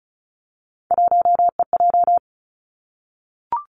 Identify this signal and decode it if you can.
Morse code, heard as a keyed tone, about 36 wpm, 700 Hz. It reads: 1EJ
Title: Three Challenging Letter-Number Combinations 35wpm